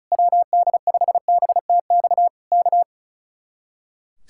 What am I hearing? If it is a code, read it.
Morse code, heard as a keyed tone, about 32 wpm, 700 Hz. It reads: WD5BTX K
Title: Speed-Racing Common QSO Phrases - 35wpm